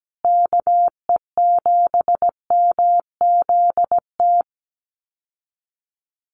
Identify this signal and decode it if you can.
Morse code, heard as a keyed tone, about 17 wpm, 700 Hz. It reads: KE7MZT
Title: USA Call Signs Repeated 2x 17wpm